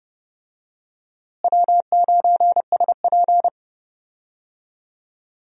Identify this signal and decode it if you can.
Morse code, heard as a keyed tone, about 32 wpm, 700 Hz. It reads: W9SP
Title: USA Call Signs - 4 Characters - Repeated 3x - 30wpm